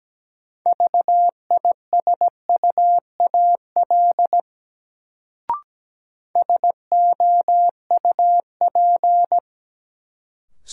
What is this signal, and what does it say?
Morse code, heard as a keyed tone, about 17 wpm, 700 Hz. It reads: VISUAL SOUP
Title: Sentences from Top 1000 Words - Review Part 6 - New Word Drill - 17wpm